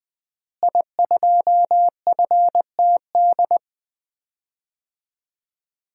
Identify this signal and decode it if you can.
Morse code, heard as a keyed tone, about 21 wpm, 700 Hz. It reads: I2FTD
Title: DXCC-100 Call Signs - 5 Characters - Repeated 3x - 20wpm